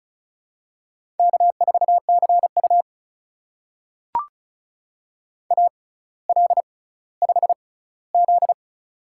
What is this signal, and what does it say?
Morse code, heard as a keyed tone, about 34 wpm, spaced as for 18 wpm, 700 Hz. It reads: K4CU AL5Z
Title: USA Call Signs - 4 Characters - 6x Character Spacing - 35wpm